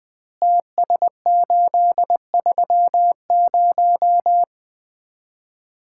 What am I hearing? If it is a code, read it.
Morse code, heard as a keyed tone, about 20 wpm, 700 Hz. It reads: TS830
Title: Common QSO Phrases - 2x Word-Spacing - 20wpm